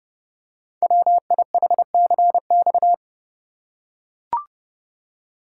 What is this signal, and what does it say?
Morse code, heard as a keyed tone, about 30 wpm, 700 Hz. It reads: WIHCX
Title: Code Groups - 5 Letters - 30wpm